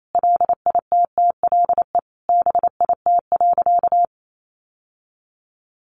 Morse code, heard as a keyed tone, about 28 wpm, 700 Hz. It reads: LITTLE BIT.
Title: Sentences from Top 900 Words 28wpm